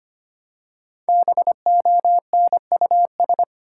Morse code, heard as a keyed tone, about 25 wpm, 700 Hz. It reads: BONUS
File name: 5 Letter Words - First 3 Letters Given - 25wpm